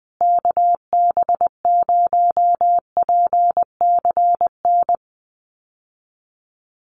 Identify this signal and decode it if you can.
Morse code, heard as a keyed tone, about 19 wpm, 700 Hz. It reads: KB0PCN
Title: USA Call Signs Repeated 2x 20wpm